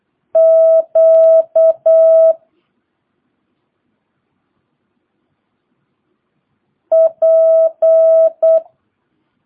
Morse code, heard as a keyed tone, about 8 wpm, 650 Hz. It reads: Q P
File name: jh8cbh / 2022cw